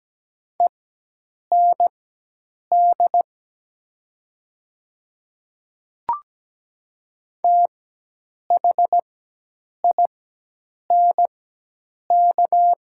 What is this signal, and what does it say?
Morse code, heard as a keyed tone, about 17 wpm, spaced as for 8 wpm, 700 Hz. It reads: END THINK
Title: 4x Character Spacing - Top 200 Words - 17wpm